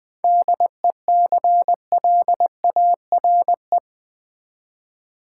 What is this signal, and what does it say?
Morse code, heard as a keyed tone, about 20 wpm, 700 Hz. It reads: DECLARE
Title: 7 Letter Words - First 3 Letters Given - 20wpm